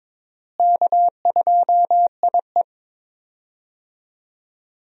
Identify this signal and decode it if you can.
Morse code, heard as a keyed tone, about 22 wpm, 700 Hz. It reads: K2IE
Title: USA Call Signs - 4 Characters - Repeated 3x - 22wpm